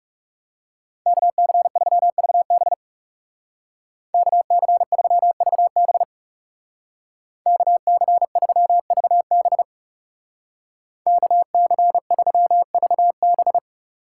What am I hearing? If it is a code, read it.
Morse code, heard as a keyed tone, about 38 wpm, 700 Hz. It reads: KC3VB KC3VB KC3VB KC3VB